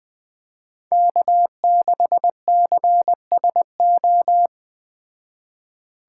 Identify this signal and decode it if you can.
Morse code, heard as a keyed tone, about 19 wpm, 700 Hz. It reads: K6CSO